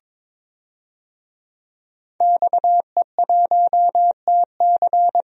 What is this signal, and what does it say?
Morse code, heard as a keyed tone, about 22 wpm, 700 Hz. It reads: XE1TC
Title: DXCC-100 Call Signs - 5 Characters - Repeated 3x - 22wpm